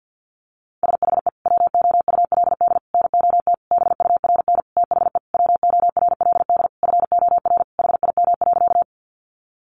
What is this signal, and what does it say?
Morse code, heard as a keyed tone, about 48 wpm, 700 Hz. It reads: SHE WOULD NOT BURN THE WORLD FOR HIM.
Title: Sentences from Top 1300 Words 50wpm